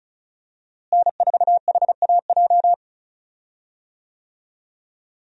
Morse code, heard as a keyed tone, about 36 wpm, 700 Hz. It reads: N4HAJ
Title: USA Call Signs - 5 Characters - Repeated 3x - 35wpm